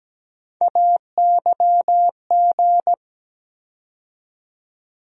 Morse code, heard as a keyed tone, about 17 wpm, 700 Hz. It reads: AYG
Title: USA Call Sign Postfixes - 3 Letters 17wpm